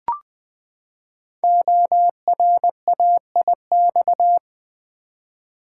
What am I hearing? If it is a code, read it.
Morse code, heard as a keyed tone, about 20 wpm, 700 Hz. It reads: ORAIX
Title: Code Groups - 5 Letters - 20wpm